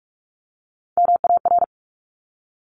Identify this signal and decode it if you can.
Morse code, heard as a keyed tone, about 46 wpm, 700 Hz. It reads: MAR
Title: ARRL Sections 45wpm